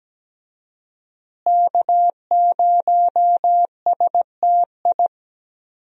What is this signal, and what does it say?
Morse code, heard as a keyed tone, about 17 wpm, 700 Hz. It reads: K0STI
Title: USA Call Signs - 5 Characters - Repeated 2x - 17wpm